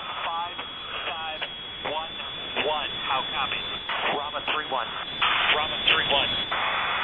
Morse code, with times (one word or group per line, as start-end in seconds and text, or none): none